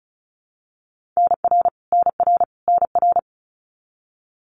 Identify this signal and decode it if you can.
Morse code, heard as a keyed tone, about 34 wpm, 700 Hz. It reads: NR NR NR
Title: Abbreviations Repeated 3x - 35wpm